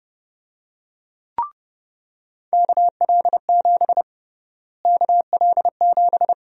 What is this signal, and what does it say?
Morse code, heard as a keyed tone, about 30 wpm, 700 Hz. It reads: KL7 KL7
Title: USA Call Sign Prefixes - Repeated 2x - 30wpm